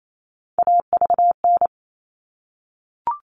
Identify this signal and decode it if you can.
Morse code, heard as a keyed tone, about 28 wpm, 700 Hz. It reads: AVN
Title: All Cut Numbers - 3-Digit 28wpm